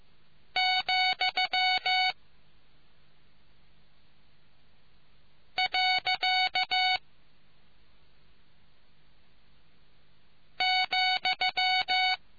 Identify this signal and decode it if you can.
Morse code, heard as a keyed tone, about 15 wpm, 2250 Hz. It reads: , . ,